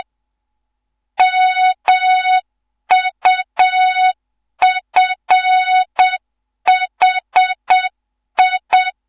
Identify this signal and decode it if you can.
Morse code, heard as a keyed tone, about 7 wpm, 3700 Hz. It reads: MUFHI
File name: Morze-HA2MN / hang / 15 GYAKORLAT